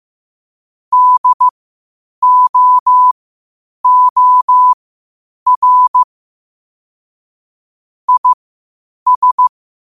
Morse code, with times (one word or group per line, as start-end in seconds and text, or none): 0.9-6.0 DOOR
8.1-9.5 IS